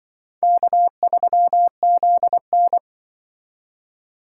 Morse code, heard as a keyed tone, about 24 wpm, 700 Hz. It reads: K3ZN
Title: Speed Racing - USA 1x2 Call Signs 20wpm